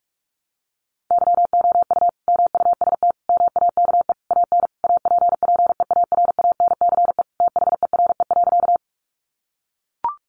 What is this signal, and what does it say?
Morse code, heard as a keyed tone, about 44 wpm, 700 Hz. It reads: YOU MUST MAKE AN APPEARANCE THERE.
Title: Sentences from Top 1100 Words 45wpm